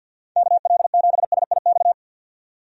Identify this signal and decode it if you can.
Morse code, heard as a keyed tone, about 52 wpm, 700 Hz. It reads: KL7SIX
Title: DXCC-100 Call Signs - 6 Characters - 50wpm